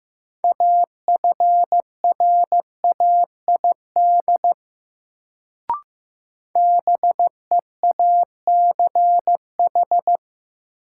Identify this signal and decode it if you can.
Morse code, heard as a keyed tone, about 15 wpm, 700 Hz. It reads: AFRAID BEACH
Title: Sentences from Top 1500 Words - New Word Drill - 15wpm